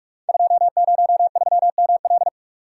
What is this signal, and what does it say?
Morse code, heard as a keyed tone, about 44 wpm, 700 Hz. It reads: 203ML
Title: Code Groups - 5 Letter:Numbers - 45wpm